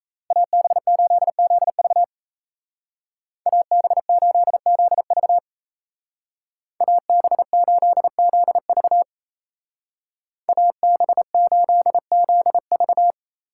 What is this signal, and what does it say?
Morse code, heard as a keyed tone, about 36 wpm, 700 Hz. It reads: AB8ZV AB8ZV AB8ZV AB8ZV